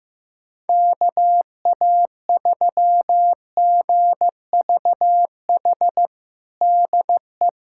Morse code, heard as a keyed tone, about 15 wpm, 700 Hz. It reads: KA3GVH DE
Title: USA Callsign DE Callsign - 6 Characters - 15wpm